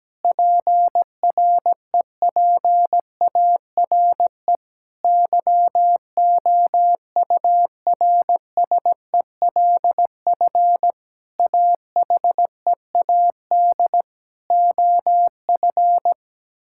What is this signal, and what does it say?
Morse code, heard as a keyed tone, about 17 wpm, 700 Hz. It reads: PREPARE YOURSELF AHEAD OF
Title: Sentences from Top 1000 Words - Review Part 6 - 17wpm